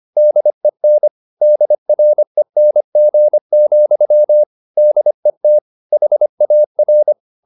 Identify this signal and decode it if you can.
Morse code, heard as a keyed tone, about 25 wpm, 600 Hz. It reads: DEN DRENG, DET HAR
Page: Ferie 25 wpm